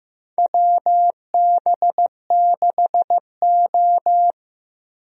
Morse code, heard as a keyed tone, about 15 wpm, 700 Hz. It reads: WB6O
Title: USA Call Signs - 4 Characters - Repeated 2x - 15wpm